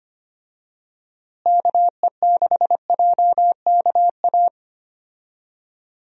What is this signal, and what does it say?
Morse code, heard as a keyed tone, about 25 wpm, 700 Hz. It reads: KE6JKA
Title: USA Call Signs - 6 Characters - Repeated 2x - 25wpm